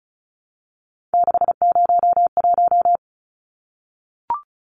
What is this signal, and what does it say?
Morse code, heard as a keyed tone, about 34 wpm, 700 Hz. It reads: 601
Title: Numbers - 3-Digit 35wpm